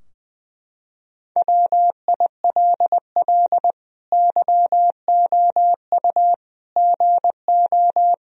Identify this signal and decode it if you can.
Morse code, heard as a keyed tone, about 21 wpm, 700 Hz. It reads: WILL YOU GO
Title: Sentences from Top 1000 Words - Review Part 2 - 20wpm